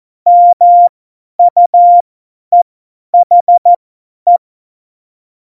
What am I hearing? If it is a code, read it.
Morse code, heard as a keyed tone, about 14 wpm, spaced as for 10 wpm, 700 Hz. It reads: MUEHE
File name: Morsekurs / Witze / Witz 11 Juli 17, Tempo 50 F70